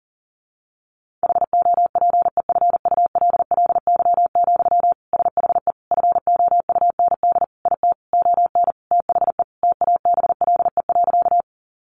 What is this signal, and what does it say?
Morse code, heard as a keyed tone, about 40 wpm, 700 Hz. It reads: HOPEFULLY, SHE FOUND IT ON THE TABLE.
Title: Sentences from Top 1500 Words 40wpm